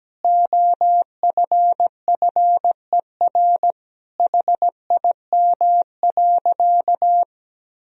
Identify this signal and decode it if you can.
Morse code, heard as a keyed tone, about 17 wpm, 700 Hz. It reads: OFFER HIM.
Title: Sentences from Top 1000 Words - Review Part 4 - 17wpm